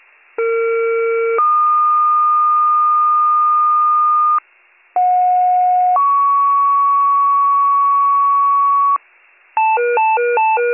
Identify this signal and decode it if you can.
Two-tone page, heard as a one-second tone, then a three-second tone, about 460 then 1220 Hz, about 730 then 1120 Hz. It is heard from 0.4 to 9.0 s.